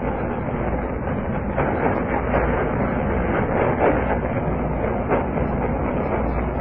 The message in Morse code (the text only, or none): none